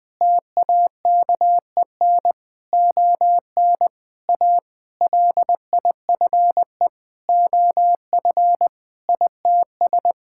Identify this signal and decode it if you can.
Morse code, heard as a keyed tone, about 21 wpm, 700 Hz. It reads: TAKEN ON A LIFE OF ITS
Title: Sentences with 8-Word Phrases - 20wpm